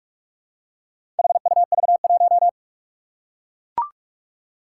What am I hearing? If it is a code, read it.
Morse code, heard as a keyed tone, about 44 wpm, 700 Hz. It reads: HUV1
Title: Four Challenging Letter-Number Combinations 45wpm